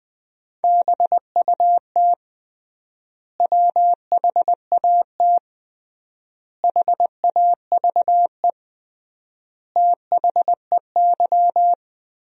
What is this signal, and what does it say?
Morse code, heard as a keyed tone, about 20 wpm, 700 Hz. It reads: BUT WHAT HAVE THEY
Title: 3x Word-Spacing - Sentences from Top 100 Words - 20wpm